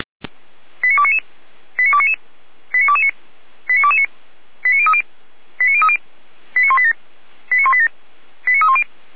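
Five-tone selective call (ZVEI), heard as quick runs of five tones, tones of about 80 ms. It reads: 89299 89299 89229 89229 89030 89030 89187 89187 89310
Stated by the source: ZVEI five-tone call 89337 received by SDR, from audio.f85360.feuerwehr-orlamuende.org fire brigade server.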